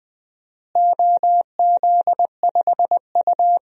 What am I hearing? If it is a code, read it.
Morse code, heard as a keyed tone, about 21 wpm, 700 Hz. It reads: OZ5U
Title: DXCC-100 Call Signs - 4 Characters - Repeated 3x - 20wpm